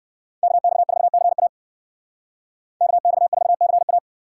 Morse code, heard as a keyed tone, about 53 wpm, 700 Hz. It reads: KX4CA KX4CA
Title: Speed Racing - USA 2x2 Call Signs 45wpm